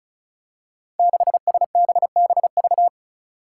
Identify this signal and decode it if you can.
Morse code, heard as a keyed tone, about 36 wpm, 700 Hz. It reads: 6SBBV
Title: Five Challenging Letter-Number Combinations 35wpm